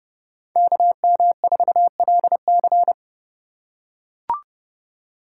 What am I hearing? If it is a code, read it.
Morse code, heard as a keyed tone, about 29 wpm, 700 Hz. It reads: KM4LC